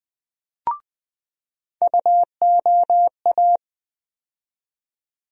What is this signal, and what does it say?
Morse code, heard as a keyed tone, about 20 wpm, 700 Hz. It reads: UOA
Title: USA Call Sign Postfixes - 3 Letters 20wpm